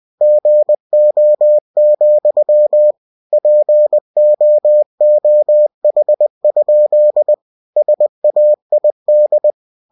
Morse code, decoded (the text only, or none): GO, POOH? SAID